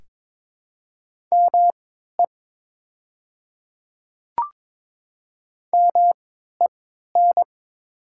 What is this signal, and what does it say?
Morse code, heard as a keyed tone, about 22 wpm, spaced as for 12 wpm, 700 Hz. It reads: ME MEN
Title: 3x Character Spacing - Top 200 Words - 22wpm